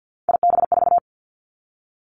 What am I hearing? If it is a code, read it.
Morse code, heard as a keyed tone, about 50 wpm, 700 Hz. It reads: IB4
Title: Three Challenging Letter-Number Combinations 50wpm